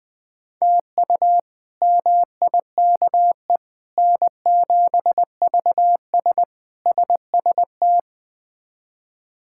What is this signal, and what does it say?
Morse code, heard as a keyed tone, about 20 wpm, 700 Hz. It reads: TU MIKE N7VS SST
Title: K1USN SST QSO Practice - 20wpm